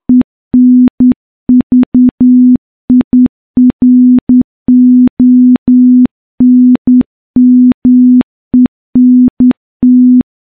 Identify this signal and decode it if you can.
Morse code, heard as a keyed tone, about 10 wpm, 250 Hz. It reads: ENVIRONMENT
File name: morse-training-10WPM